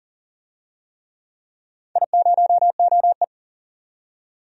Morse code, heard as a keyed tone, about 40 wpm, 700 Hz. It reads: I0OE
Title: DXCC-100 Call Signs - 4 Characters - 40wpm